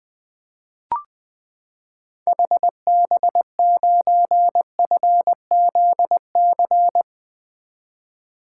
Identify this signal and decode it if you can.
Morse code, heard as a keyed tone, about 20 wpm, 700 Hz. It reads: HB9FZC
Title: 2023 SOTA Activator Call Signs - 20wpm